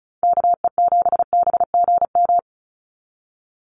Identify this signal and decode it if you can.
Morse code, heard as a keyed tone, about 34 wpm, 700 Hz. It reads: KE7BGM
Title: NA SOTA Activator Callsigns 35wpm